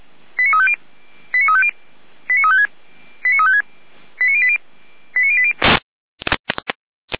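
Five-tone selective call (ZVEI), heard as quick runs of five tones, tones of about 70 ms. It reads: 89350 89350 89356 89356 89080 89080
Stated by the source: ZVEI five-tone selective call received on an SDR.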